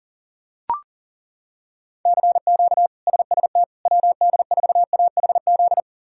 Morse code, heard as a keyed tone, about 39 wpm, 700 Hz. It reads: CQ SST WD4AHZ
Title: K1USN SST QSO Practice - 40wpm